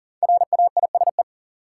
Morse code, heard as a keyed tone, about 40 wpm, 700 Hz. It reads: RAISE
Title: Five Letter Words - 40wpm